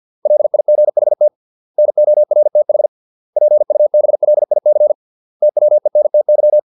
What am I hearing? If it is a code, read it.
Morse code, heard as a keyed tone, about 50 wpm, 600 Hz. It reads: LIGHT NORTH PUBLIC TWENTY